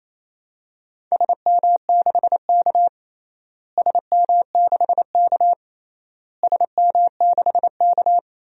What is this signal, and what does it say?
Morse code, heard as a keyed tone, about 28 wpm, 700 Hz. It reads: SM6K SM6K SM6K
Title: DXCC-100 Call Signs - 4 Characters - Repeated 3x - 28wpm